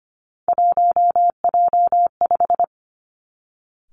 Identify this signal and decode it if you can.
Morse code, heard as a keyed tone, about 25 wpm, 700 Hz. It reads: 1J5
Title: Three Challenging Letter-Number Combinations 25wpm